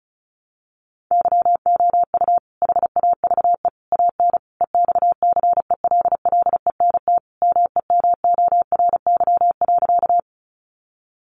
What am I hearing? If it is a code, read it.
Morse code, heard as a keyed tone, about 34 wpm, 700 Hz. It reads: YOU HAVE AN EXCELLENT MEMORY.
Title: Sentences from Top 1300 Words 35wpm